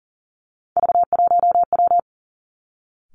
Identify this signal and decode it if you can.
Morse code, heard as a keyed tone, about 40 wpm, 700 Hz. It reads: V1W
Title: Three Challenging Letter-Number Combinations 40wpm